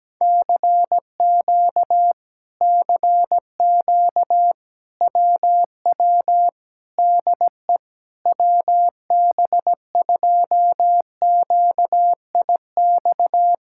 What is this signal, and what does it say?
Morse code, heard as a keyed tone, about 17 wpm, 700 Hz. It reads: CQ CQ WW DE WB2QIX
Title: Speed-Racing Common QSO Phrases - 17wpm